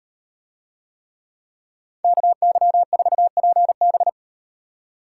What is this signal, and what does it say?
Morse code, heard as a keyed tone, about 38 wpm, 700 Hz. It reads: KY4PB